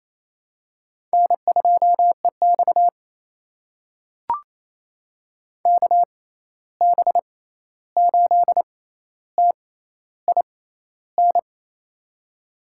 Morse code, heard as a keyed tone, about 28 wpm, spaced as for 13 wpm, 700 Hz. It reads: N2EX KB8TIN
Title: USA Call Signs - 6x Character Spacing - 28wpm